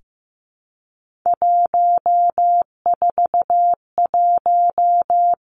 Morse code, heard as a keyed tone, about 15 wpm, 700 Hz. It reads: 141
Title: Three Challenging Letter-Number Combinations 15wpm